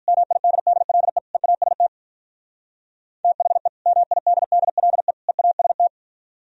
Morse code, heard as a keyed tone, about 44 wpm, 700 Hz. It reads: MIDDLE EAST THE MIDDLE EAST